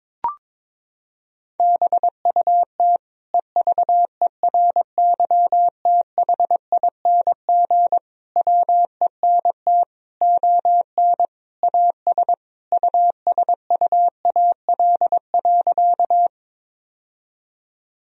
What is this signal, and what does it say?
Morse code, heard as a keyed tone, about 21 wpm, 700 Hz. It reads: BUT EVERYTHING WENT ON AS USUAL.
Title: Sentences from Top 1000 Words - Review Part 2 - 22wpm